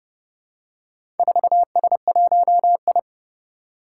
Morse code, heard as a keyed tone, about 31 wpm, 700 Hz. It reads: 4S1I
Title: Four Challenging Letter-Number Combinations 30wpm